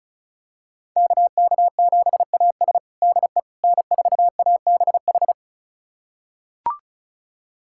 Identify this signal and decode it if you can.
Morse code, heard as a keyed tone, about 36 wpm, 700 Hz. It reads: KK7AS DE N4ABH
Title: USA Callsign DE Callsign - 5 Characters - 35wpm